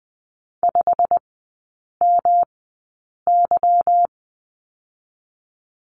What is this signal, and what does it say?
Morse code, heard as a keyed tone, about 20 wpm, spaced as for 8 wpm, 690 Hz. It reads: 5MY
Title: 145 Lesson 24e - Letter y - Three Characters 20wpm